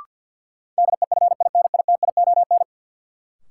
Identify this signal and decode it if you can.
Morse code, heard as a keyed tone, about 51 wpm, 700 Hz. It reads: DEFINITION
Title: Sentences from Top 1000 Words - Review Part 4 - New Word Drill - 50wpm